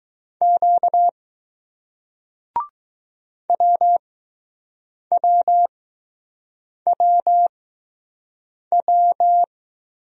Morse code, heard as a keyed tone, about 20 wpm, spaced as for 6 wpm, 700 Hz. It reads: Q WWWW